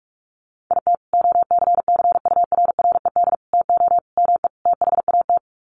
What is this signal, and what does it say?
Morse code, heard as a keyed tone, about 44 wpm, 700 Hz. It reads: IT OCCURRED TO ME THAT
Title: Sets of 5 Words Encore 45wpm